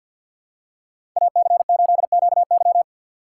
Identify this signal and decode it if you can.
Morse code, heard as a keyed tone, about 48 wpm, 700 Hz. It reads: AC8QY
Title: USA Call Signs - 5 Characters - 2x Character Spacing - 50wpm